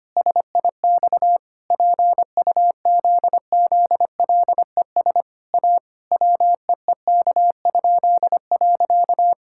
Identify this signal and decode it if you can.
Morse code, heard as a keyed tone, about 25 wpm, 700 Hz. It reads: SIX PUZZLES A WEEK?.
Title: Pangram Challenge v2 - 25wpm